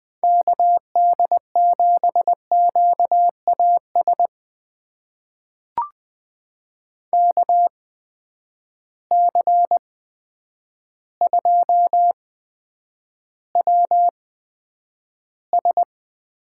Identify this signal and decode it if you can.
Morse code, heard as a keyed tone, about 20 wpm, spaced as for 9 wpm, 700 Hz. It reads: KD7QAS KC2WS
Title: USA Call Signs - 6 Characters - 8x Character Spacing - 20wpm